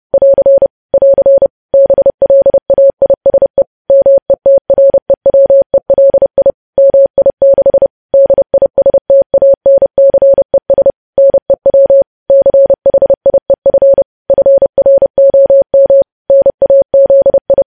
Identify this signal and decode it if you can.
Morse code, heard as a keyed tone, about 30 wpm, 550 Hz. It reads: <AR> <AR> BLAISE METREWELI MI6 DISTANCES NEW CHIEF FROM NAZI
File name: podcast / News headlines-1x-30x30